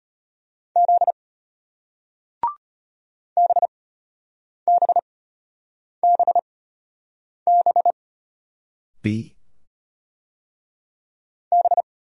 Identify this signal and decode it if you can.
Morse code, heard as a keyed tone, about 33 wpm, spaced as for 8 wpm, 690 Hz. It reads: Z BBBB B